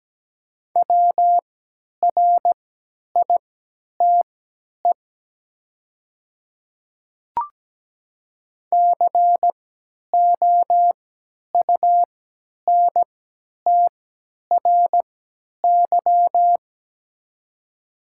Morse code, heard as a keyed tone, about 17 wpm, spaced as for 10 wpm, 700 Hz. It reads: WRITE COUNTRY
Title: 3x Character Spacing - Top 500 Words - 17wpm